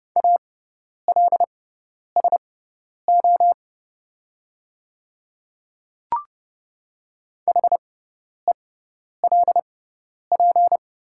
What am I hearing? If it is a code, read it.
Morse code, heard as a keyed tone, about 30 wpm, spaced as for 10 wpm, 700 Hz. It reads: ALSO HELP
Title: 6x Character Spacing - Top 200 Words - 30wpm